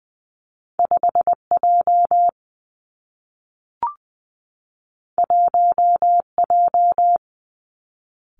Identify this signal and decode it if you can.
Morse code, heard as a keyed tone, about 20 wpm, 700 Hz. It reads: 5 J 1 J
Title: Two Challenging Letter-Number Combinations 20wpm